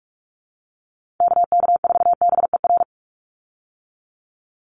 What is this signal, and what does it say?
Morse code, heard as a keyed tone, about 44 wpm, 700 Hz. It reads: KK4BER
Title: USA Call Signs 45wpm